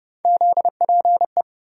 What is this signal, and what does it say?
Morse code, heard as a keyed tone, about 29 wpm, 700 Hz. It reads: ZPE
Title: USA Call Sign Postfixes - 3 Letters 30wpm